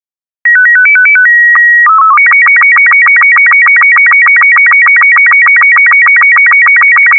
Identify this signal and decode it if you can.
SSTV (slow-scan television), Robot 36.
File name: cf / SSTV Logo Wikipedia Robot36